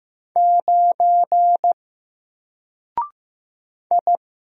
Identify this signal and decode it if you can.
Morse code, heard as a keyed tone, about 15 wpm, 700 Hz. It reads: 9 I
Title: Single Character (Letter:Number:Punctuation:Prosign) 15wpm